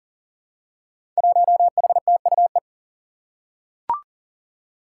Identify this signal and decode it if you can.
Morse code, heard as a keyed tone, about 40 wpm, 700 Hz. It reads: 1HTUE
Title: Five Challenging Letter-Number Combinations 40wpm